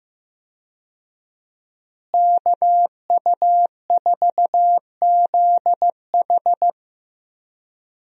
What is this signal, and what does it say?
Morse code, heard as a keyed tone, about 15 wpm, 700 Hz. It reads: KU4ZH
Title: USA Call Signs - 5 Characters - Repeated 3x - 15wpm